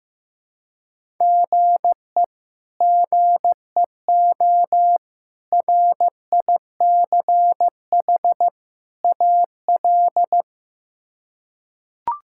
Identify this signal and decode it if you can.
Morse code, heard as a keyed tone, about 15 wpm, 700 Hz. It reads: GE GEO RICH AL
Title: K1USN SST QSO Practice - 15wpm